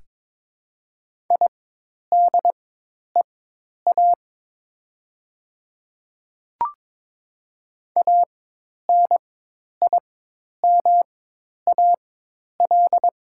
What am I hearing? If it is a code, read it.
Morse code, heard as a keyed tone, about 22 wpm, spaced as for 10 wpm, 700 Hz. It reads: IDEA ANIMAL
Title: 4x Character Spacing - Top 300 Words - 22wpm